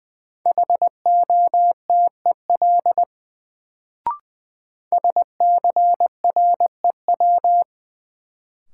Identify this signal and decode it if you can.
Morse code, heard as a keyed tone, about 21 wpm, 700 Hz. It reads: HOTEL SCREW
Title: Sentences from Top 1600 Words - New Word Drill Alt - 20wpm